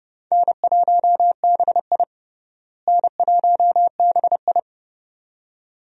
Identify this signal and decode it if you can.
Morse code, heard as a keyed tone, about 32 wpm, 700 Hz. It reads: N1BI N1BI
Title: USA Call Signs - 4 Characters - Repeated 2x - 30wpm